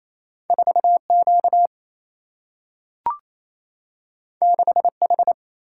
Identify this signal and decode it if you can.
Morse code, heard as a keyed tone, about 28 wpm, 700 Hz. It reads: 4Q 6H
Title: Single-Letter Single-Number 28wpm